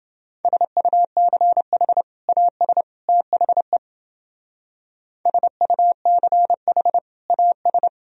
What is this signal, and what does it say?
Morse code, heard as a keyed tone, about 30 wpm, 700 Hz. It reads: SUCH AS THE SUCH AS